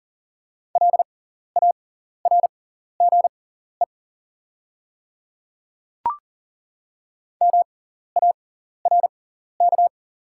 Morse code, heard as a keyed tone, about 40 wpm, spaced as for 14 wpm, 700 Hz. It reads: LARGE MARK
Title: 6x Character Spacing - Top 500 Words - 40wpm